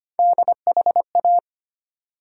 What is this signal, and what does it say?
Morse code, heard as a keyed tone, about 25 wpm, 700 Hz. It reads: DHA
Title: USA Call Sign Postfixes - 3 Letters 25wpm